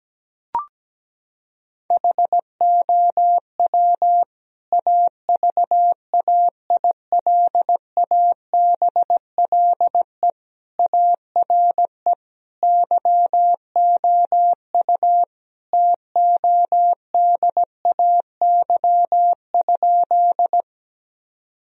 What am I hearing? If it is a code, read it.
Morse code, heard as a keyed tone, about 17 wpm, 700 Hz. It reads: HOW AVAILABLE ARE YOU TODAY?
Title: Sentences from Top 1000 Words - Review Part 3 - 17wpm